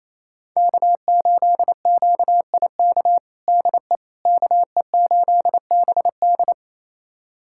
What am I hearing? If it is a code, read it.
Morse code, heard as a keyed tone, about 28 wpm, 700 Hz. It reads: K8QIK DE KE8BD
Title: USA Callsign DE Callsign - 5 Characters - 28wpm